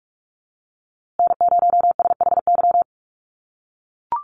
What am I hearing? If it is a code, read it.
Morse code, heard as a keyed tone, about 44 wpm, 700 Hz. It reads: N0SHY
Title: USA Call Signs Repeated 2x 45wpm